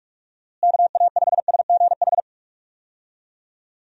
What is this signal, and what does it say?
Morse code, heard as a keyed tone, about 46 wpm, 700 Hz. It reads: KA5SGH
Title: USA Call Signs - 6 Characters - Repeated 2x - 45wpm